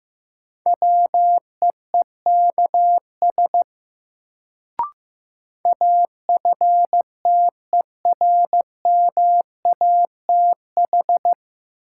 Morse code, heard as a keyed tone, about 15 wpm, 700 Hz. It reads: WEEKS AFTERMATH
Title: Sets of 5 Words - New Word Drill - 15wpm